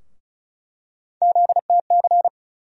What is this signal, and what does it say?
Morse code, heard as a keyed tone, about 36 wpm, 700 Hz. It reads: ZTC
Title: USA Call Sign Postfixes - 3 Letters 35wpm